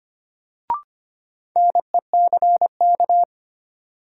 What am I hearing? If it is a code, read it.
Morse code, heard as a keyed tone, about 25 wpm, 700 Hz. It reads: NECK
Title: Sets of 5 Words Encore - New Word Drill - 25wpm